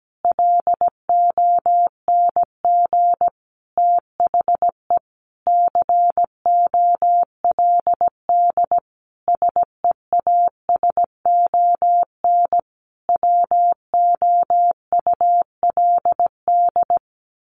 Morse code, heard as a keyed tone, about 17 wpm, 700 Hz. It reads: LONG THE COLD SEASON WOULD
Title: Sentences from Top 1400 Words 17wpm